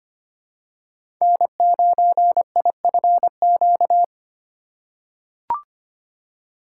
Morse code, heard as a keyed tone, about 25 wpm, 700 Hz. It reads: N9IFQ
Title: USA Call Signs - 2x Character Spacing - 25wpm